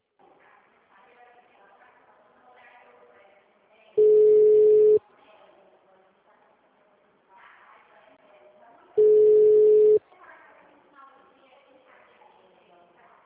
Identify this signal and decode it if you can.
Ringback tone (Europe, 425 Hz), with speech.